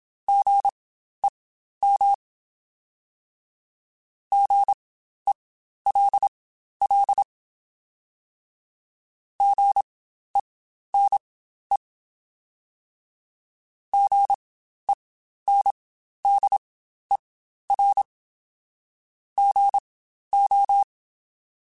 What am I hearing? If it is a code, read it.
Morse code, heard as a keyed tone, about 27 wpm, spaced as for 12 wpm, 800 Hz. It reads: GEM GELL GENE GENDER GO
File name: Spaced Alphas / Spaced Alphas mp3s / Gs sp @25